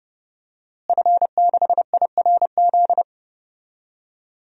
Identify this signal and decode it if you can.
Morse code, heard as a keyed tone, about 30 wpm, 700 Hz. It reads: F6IRZ
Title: DXCC-100 Call Signs - 5 Characters - Repeated 2x - 30wpm